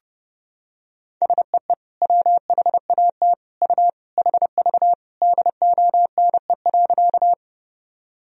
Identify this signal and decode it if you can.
Morse code, heard as a keyed tone, about 30 wpm, 700 Hz. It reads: SEE WHAT U HV DONE.
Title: Sentences from Top 500 Words and Abbreviations 30wpm